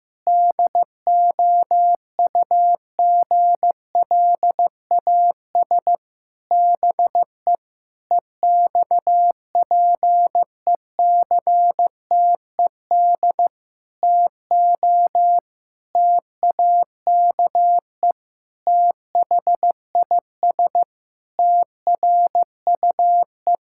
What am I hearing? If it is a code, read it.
Morse code, heard as a keyed tone, about 14 wpm, 700 Hz. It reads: DOUGLAS BE EXPECTED TO TAKE THIS TRUE
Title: Pangram Challenge v2 - 15wpm